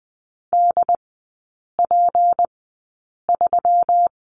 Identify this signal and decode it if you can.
Morse code, heard as a keyed tone, about 20 wpm, spaced as for 8 wpm, 700 Hz. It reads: DP3
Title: 152 Lesson 25e - Letter p - Three Characters 20wpm